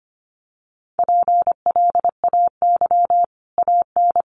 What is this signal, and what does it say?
Morse code, heard as a keyed tone, about 25 wpm, 700 Hz. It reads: PLAY AN
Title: Sets of 5 Words Encore 25wpm